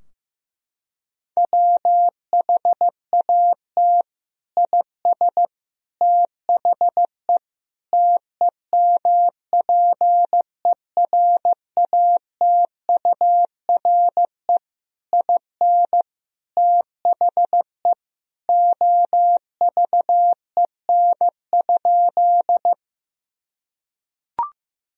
Morse code, heard as a keyed tone, about 16 wpm, 700 Hz. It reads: WHAT IS THE TEMPERATURE IN THE OVEN?
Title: Sentences from Top 1000 Words - Review Part 4 - 15wpm